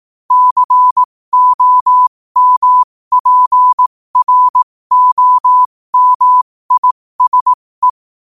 Morse code, heard as a keyed tone, about 18 wpm, spaced as for 15 wpm, 1000 Hz. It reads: COMPROMISE
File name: qotdfiles / 15wpm